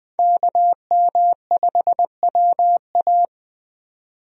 Morse code, heard as a keyed tone, about 20 wpm, 700 Hz. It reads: KM5WA
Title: USA Call Signs - 5 Characters - 20wpm